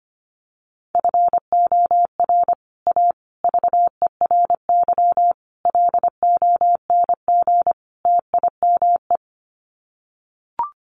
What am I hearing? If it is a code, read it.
Morse code, heard as a keyed tone, about 25 wpm, 700 Hz. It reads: FOR A VERY LONG TIME
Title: Sets of 5 Words 25wpm